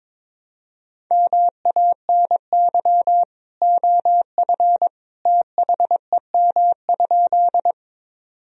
Morse code, heard as a keyed tone, about 22 wpm, 700 Hz. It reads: MANY OF THEM?
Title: Sentences from Top 1000 Words - Review Part 1 - 22wpm